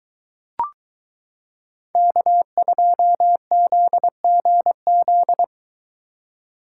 Morse code, heard as a keyed tone, about 22 wpm, 700 Hz. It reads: K2ZGZ